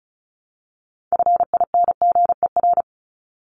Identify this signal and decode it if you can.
Morse code, heard as a keyed tone, about 34 wpm, 700 Hz. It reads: FINGER
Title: Top 1501-1750 Words 35wpm